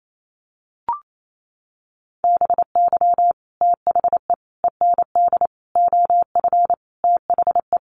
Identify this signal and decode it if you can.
Morse code, heard as a keyed tone, about 28 wpm, 700 Hz. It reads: BY THE END OF THE
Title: Sets of 5 Words 28wpm